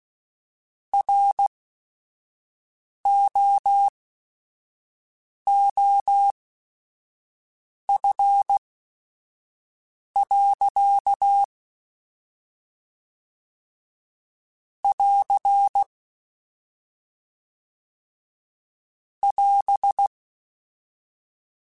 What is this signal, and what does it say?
Morse code, heard as a keyed tone, about 16 wpm, spaced as for 5 wpm, 800 Hz. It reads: ROOF. <AR> <AS>
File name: Weather Sets / Weather Sets-mp3s / Weather 4 28 @5